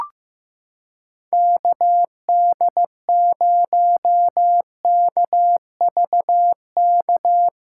Morse code, heard as a keyed tone, about 16 wpm, 700 Hz. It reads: KD0KVK
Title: USA Callsign DE Callsign - 6 Characters - 15wpm